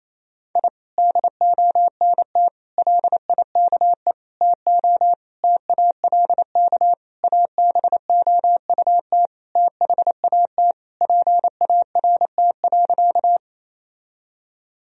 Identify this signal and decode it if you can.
Morse code, heard as a keyed tone, about 27 wpm, 700 Hz. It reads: I DONT LIKE TO TALK ABOUT THAT PART.
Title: Sentences from Top 1000 Words - Review Part 1 - 28wpm